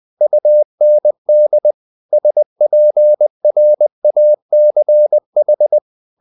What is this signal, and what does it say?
Morse code, heard as a keyed tone, about 20 wpm, 600 Hz. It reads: UND SPRACH